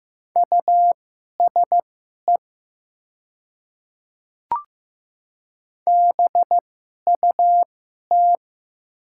Morse code, heard as a keyed tone, about 15 wpm, spaced as for 11 wpm, 700 Hz. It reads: USE BUT
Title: 2x Character Spacing - Top 100 Words - 15wpm